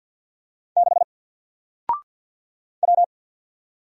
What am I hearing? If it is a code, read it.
Morse code, heard as a keyed tone, about 53 wpm, 700 Hz. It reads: 6 W
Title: Single Letter-Number 50wpm